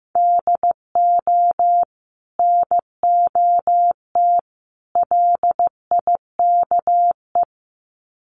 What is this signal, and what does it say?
Morse code, heard as a keyed tone, about 15 wpm, 700 Hz. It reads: DO NOT LIKE
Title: Sets of 3 Words 15wpm